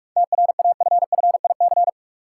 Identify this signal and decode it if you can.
Morse code, heard as a keyed tone, about 46 wpm, 700 Hz. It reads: TRAFFIC
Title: Seven Letter Words - 45wpm